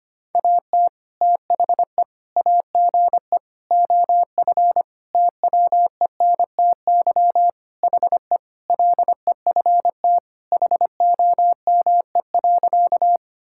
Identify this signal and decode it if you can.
Morse code, heard as a keyed tone, about 25 wpm, 700 Hz. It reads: AT THE AGE OF TWENTY HE LEFT HOME.
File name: practice / 25wpm / Sentences with 4-Word Phrases - 25wpm